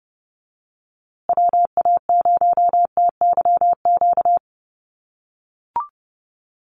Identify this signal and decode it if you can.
Morse code, heard as a keyed tone, about 30 wpm, 700 Hz. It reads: WA0TYQ
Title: USA Call Signs Repeated 2x 30wpm